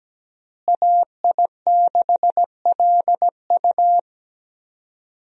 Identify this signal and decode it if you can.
Morse code, heard as a keyed tone, about 17 wpm, 700 Hz. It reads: AI6LU
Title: USA Call Signs - 5 Characters - Repeated 3x - 17wpm